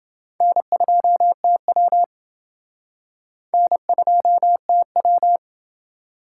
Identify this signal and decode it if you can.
Morse code, heard as a keyed tone, about 28 wpm, 700 Hz. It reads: N2TW N2TW